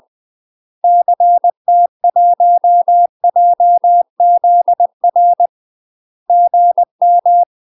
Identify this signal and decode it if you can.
Morse code, heard as a keyed tone, about 20 wpm, 700 Hz. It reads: CT1JZR GM
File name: UK MASTER QSOs TRAINING / 20WPM / UK MASTER QSOs 0038